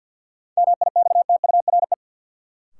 Morse code, heard as a keyed tone, about 48 wpm, 700 Hz. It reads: MIXTURE
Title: Sentences from Top 1500 Words - New Word Drill - 50wpm